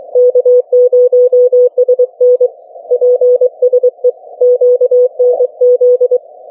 Morse code, heard as a keyed tone, about 23 wpm, 500 Hz. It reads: K0SN PSE QNZ